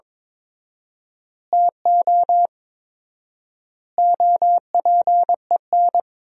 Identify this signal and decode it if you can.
Morse code, heard as a keyed tone, about 22 wpm, 700 Hz. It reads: TO OPEN
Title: 4x Word-Spacing - Sentences from Top 300 Words - 22wpm